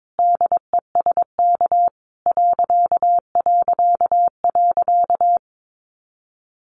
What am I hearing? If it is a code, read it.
Morse code, heard as a keyed tone, about 22 wpm, 700 Hz. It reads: DESK ...
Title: A Sound Of Thunder 22wpm